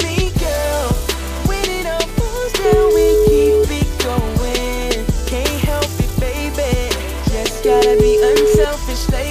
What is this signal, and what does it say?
Ringback tone (Europe, 425 Hz), with noise.